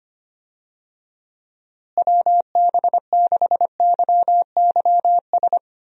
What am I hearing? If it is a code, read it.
Morse code, heard as a keyed tone, about 25 wpm, 700 Hz. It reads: WB6YYS
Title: USA Call Signs - 6 Characters - Repeated 2x - 25wpm